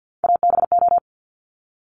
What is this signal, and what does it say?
Morse code, heard as a keyed tone, about 50 wpm, 700 Hz. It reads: ABO
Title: Top Three Letter Combinations in Words 50wpm